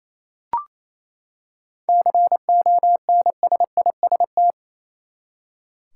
Morse code, heard as a keyed tone, about 28 wpm, 700 Hz. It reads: CONSIST
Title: Seven Letter Words - 28wpm